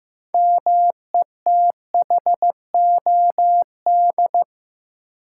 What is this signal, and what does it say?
Morse code, heard as a keyed tone, about 15 wpm, 700 Hz. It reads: METHOD
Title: Sentences from Top 600 Words - New Word Drill - 15wpm